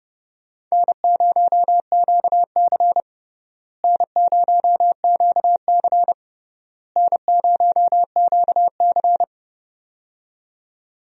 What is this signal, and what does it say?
Morse code, heard as a keyed tone, about 32 wpm, 700 Hz. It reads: N0QC N0QC N0QC
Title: USA Call Signs - 4 Characters - Repeated 3x - 30wpm